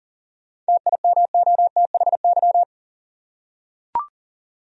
Morse code, heard as a keyed tone, about 40 wpm, 700 Hz. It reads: TIMOTHY
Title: USA Common Names - 40wpm